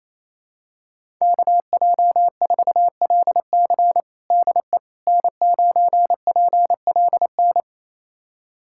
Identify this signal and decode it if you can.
Morse code, heard as a keyed tone, about 28 wpm, 700 Hz. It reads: KJ4LC DE N9PLN
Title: USA Callsign DE Callsign - 5 Characters - 28wpm